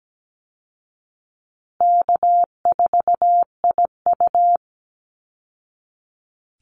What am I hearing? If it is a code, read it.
Morse code, heard as a keyed tone, about 17 wpm, 700 Hz. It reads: K4IU
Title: Easy Call Signs Repeated 3x - 17wpm